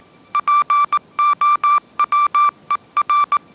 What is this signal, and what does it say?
Morse code, heard as a keyed tone, about 19 wpm, 1170 Hz. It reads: POWER
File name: mrx083 20wpm